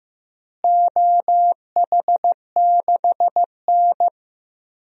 Morse code, heard as a keyed tone, about 15 wpm, 700 Hz. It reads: OH6N
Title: DXCC-100 Call Signs - 4 Characters - Repeated 3x - 15wpm